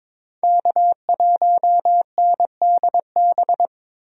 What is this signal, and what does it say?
Morse code, heard as a keyed tone, about 22 wpm, 700 Hz. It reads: K1NDB
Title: USA Call Signs - 5 Characters - 22wpm